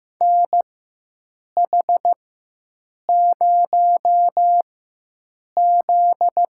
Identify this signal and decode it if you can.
Morse code, heard as a keyed tone, about 15 wpm, spaced as for 7 wpm, 700 Hz. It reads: NH0Z
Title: USA Call Signs - 4 Characters - 4x Character Spacing - 15wpm